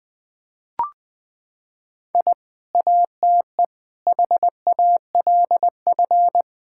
Morse code, heard as a keyed tone, about 21 wpm, 700 Hz. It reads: I ATE HALF
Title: Sentences with 3-Word Phrases - 20wpm